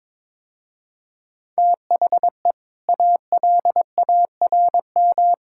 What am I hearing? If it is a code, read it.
Morse code, heard as a keyed tone, about 22 wpm, 700 Hz. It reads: THE ALARM
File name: practice / 22wpm / Easy Sentences with Hints - 22wpm